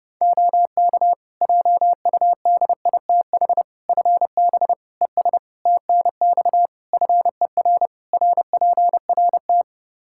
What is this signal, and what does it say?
Morse code, heard as a keyed tone, about 28 wpm, 700 Hz. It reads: OK JUDITH FB ES TNX FER RPRT
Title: Practice QSO - Exchange 2 - Info - 30wpm